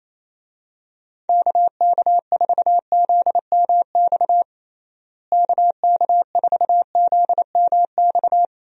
Morse code, heard as a keyed tone, about 27 wpm, 700 Hz. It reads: KK4ZMX KK4ZMX
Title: USA Call Signs - 6 Characters - Repeated 2x - 28wpm